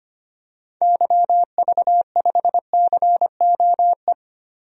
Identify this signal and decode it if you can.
Morse code, heard as a keyed tone, about 25 wpm, 700 Hz. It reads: YV5COE
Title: DXCC-100 Call Signs - 6 Characters - Repeated 3x - 25wpm